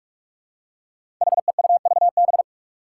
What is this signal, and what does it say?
Morse code, heard as a keyed tone, about 46 wpm, 700 Hz. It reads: HEUVB
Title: Five Challenging Letter-Number Combinations 45wpm